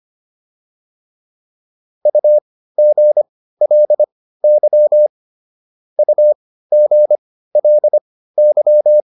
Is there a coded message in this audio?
Yes: Morse code from 2.1 to 9.0 s, heard as a keyed tone, about 25 wpm, spaced as for 15 wpm, 600 Hz. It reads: UGLY UGLY